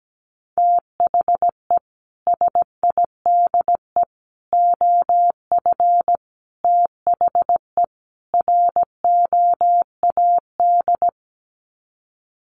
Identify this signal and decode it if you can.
Morse code, heard as a keyed tone, about 17 wpm, 700 Hz. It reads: THE SIDE OF THE ROAD
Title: Sets of 5 Words 17wpm